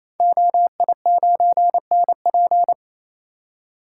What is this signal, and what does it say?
Morse code, heard as a keyed tone, about 28 wpm, 700 Hz. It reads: OI9NP
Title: Code Groups - 5 Letter:Numbers - 28wpm